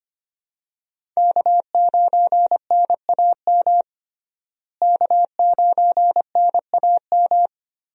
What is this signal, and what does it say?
Morse code, heard as a keyed tone, about 25 wpm, 700 Hz. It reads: K9NAM K9NAM
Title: USA Call Signs - 5 Characters - Repeated 3x - 25wpm